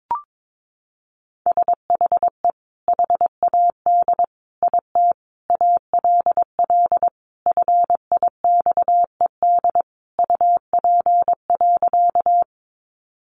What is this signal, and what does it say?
Morse code, heard as a keyed tone, about 22 wpm, 700 Hz. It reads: SHE HAD IT ALL FIXED UP.
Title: Sentences from Top 1100 Words 22wpm